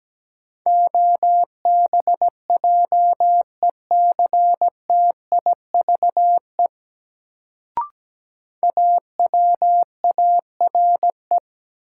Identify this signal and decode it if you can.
Morse code, heard as a keyed tone, about 17 wpm, 700 Hz. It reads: OBJECTIVE AWARE
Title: Sentences from Top 1200 Words - New Word Drill - 17wpm